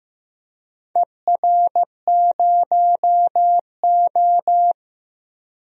Morse code, heard as a keyed tone, about 15 wpm, 700 Hz. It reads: ER0O
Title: DXCC-100 Call Signs - 4 Characters - 15wpm